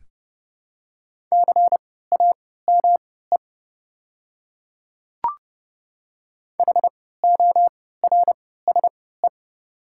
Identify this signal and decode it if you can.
Morse code, heard as a keyed tone, about 30 wpm, spaced as for 17 wpm, 690 Hz. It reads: CAME HORSE
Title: 3x Character Spacing - Top 300 Words - 30wpm